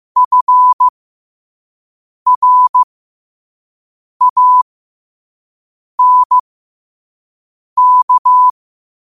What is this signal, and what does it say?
Morse code, heard as a keyed tone, about 15 wpm, spaced as for 5 wpm, 1000 Hz. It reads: FRANK
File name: qotdfiles / 5wpm